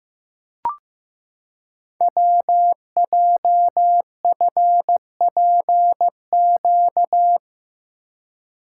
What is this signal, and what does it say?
Morse code, heard as a keyed tone, about 15 wpm, 700 Hz. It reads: WJFPQ
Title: Code Groups - 5 Letters - 15wpm